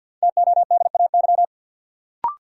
Morse code, heard as a keyed tone, about 53 wpm, 700 Hz. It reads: TODAY
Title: Sets of 5 Words Encore - New Word Drill - 50wpm